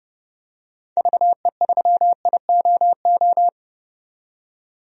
Morse code, heard as a keyed tone, about 30 wpm, 700 Hz. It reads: VE3IOO
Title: DXCC-100 Call Signs - 6 Characters - 30wpm